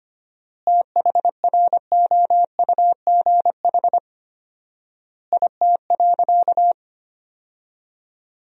Morse code, heard as a keyed tone, about 25 wpm, 700 Hz. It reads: THROUGH IT.
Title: 4x Word-Spacing - Sentences from Top 200 Words - 25wpm